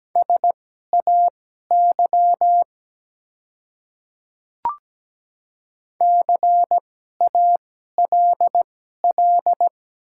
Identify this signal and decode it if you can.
Morse code, heard as a keyed tone, about 17 wpm, spaced as for 12 wpm, 700 Hz. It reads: SAY CALL
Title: 2x Character Spacing - Top 200 Words - 17wpm